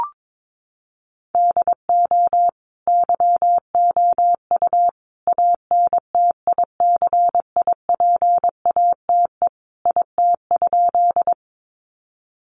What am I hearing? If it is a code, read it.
Morse code, heard as a keyed tone, about 22 wpm, 700 Hz. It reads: DO YOU ANTICIPATE IT?
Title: Sentences from Top 1500 Words 22wpm